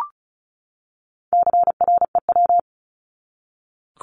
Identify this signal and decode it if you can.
Morse code, heard as a keyed tone, about 34 wpm, 700 Hz. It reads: CREW
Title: Top 1251-1500 Words 35wpm